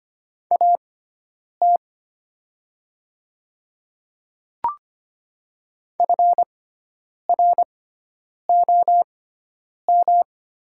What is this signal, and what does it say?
Morse code, heard as a keyed tone, about 25 wpm, spaced as for 9 wpm, 700 Hz. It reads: AT FROM
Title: 6x Character Spacing - Top 100 Words - 25wpm